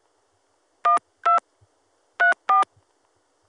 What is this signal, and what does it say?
DTMF, heard as telephone touch tones, tones of about 140 ms. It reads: 12 34